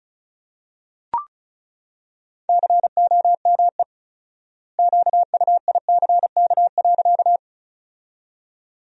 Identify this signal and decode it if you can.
Morse code, heard as a keyed tone, about 36 wpm, 700 Hz. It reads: COME QUICK.
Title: 4x Word-Spacing - Sentences from Top 500 Words - 35wpm